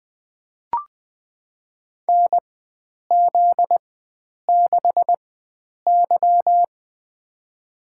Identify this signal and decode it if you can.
Morse code, heard as a keyed tone, about 20 wpm, spaced as for 9 wpm, 700 Hz. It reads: NZ6Y
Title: USA Call Signs - 4 Characters - 4x Character Spacing - 20wpm